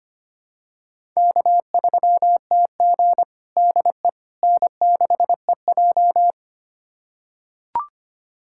Morse code, heard as a keyed tone, about 25 wpm, 700 Hz. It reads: K3TG DE N6EJ
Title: USA Callsign DE Callsign - 4 Characters - 25wpm